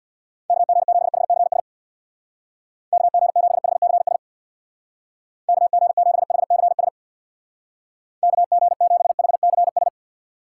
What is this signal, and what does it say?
Morse code, heard as a keyed tone, about 62 wpm, 700 Hz. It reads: KG7HCS KG7HCS KG7HCS KG7HCS
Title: Speed Racing - USA 2x3 Call Signs 50wpm